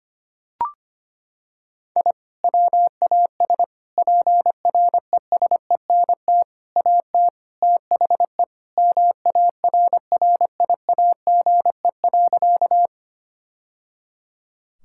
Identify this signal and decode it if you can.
Morse code, heard as a keyed tone, about 25 wpm, 700 Hz. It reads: I WAS PRESENT AT THE MARRIAGE.
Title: Sentences from Top 1000 Words - Review Part 5 - 25wpm